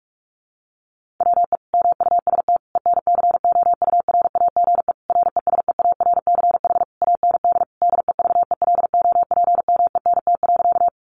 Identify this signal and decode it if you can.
Morse code, heard as a keyed tone, about 44 wpm, 700 Hz. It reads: WE MUST ENCOURAGE RESEARCH AND DEVELOPMENT.
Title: Sentences from Top 1400 Words 45wpm